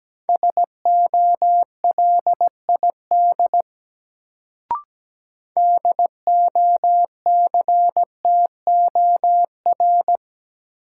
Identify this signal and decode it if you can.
Morse code, heard as a keyed tone, about 17 wpm, 700 Hz. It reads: SOLID DOCTOR
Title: Sentences from Top 1500 Words - New Word Drill Alt - 17wpm